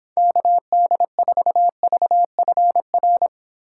Morse code, heard as a keyed tone, about 26 wpm, 700 Hz. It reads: KD4VFR